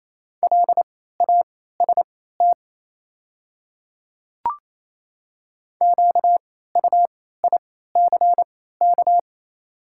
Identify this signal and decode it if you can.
Morse code, heard as a keyed tone, about 27 wpm, spaced as for 16 wpm, 690 Hz. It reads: LAST QUICK
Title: 3x Character Spacing - Top 500 Words - 28wpm